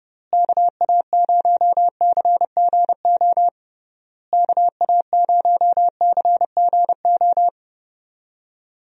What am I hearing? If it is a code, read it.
Morse code, heard as a keyed tone, about 30 wpm, 700 Hz. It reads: KA0CGO KA0CGO
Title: USA Call Signs - 6 Characters - Repeated 3x - 30wpm